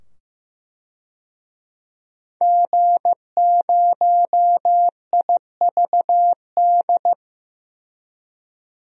Morse code, heard as a keyed tone, about 16 wpm, 700 Hz. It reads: G0IVD
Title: DXCC-100 Call Signs - 5 Characters - 15wpm